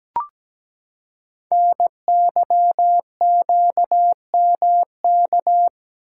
Morse code, heard as a keyed tone, about 17 wpm, 700 Hz. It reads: NYQMK
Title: Code Groups - 5 Letters - 17wpm